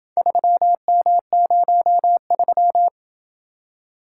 Morse code, heard as a keyed tone, about 27 wpm, 700 Hz. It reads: 3M03